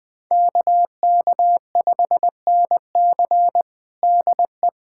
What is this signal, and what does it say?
Morse code, heard as a keyed tone, about 21 wpm, 700 Hz. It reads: KK5NC DE
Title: USA Callsign DE Callsign 20wpm